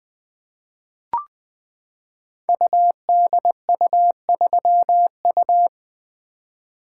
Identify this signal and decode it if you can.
Morse code, heard as a keyed tone, about 21 wpm, 700 Hz. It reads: UDU3U
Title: Code Groups - 5 Letter:Numbers - 20wpm